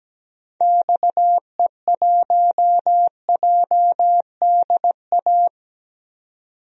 Morse code, heard as a keyed tone, about 17 wpm, 700 Hz. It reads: XE1JDA
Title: DXCC-100 Call Signs - 6 Characters - 17wpm